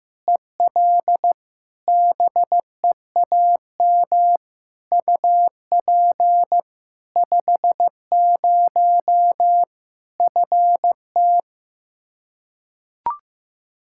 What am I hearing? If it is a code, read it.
Morse code, heard as a keyed tone, about 15 wpm, 700 Hz. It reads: EL BEAM UP 50 FT
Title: Practice QSO - Exchange 2 - Info - 15wpm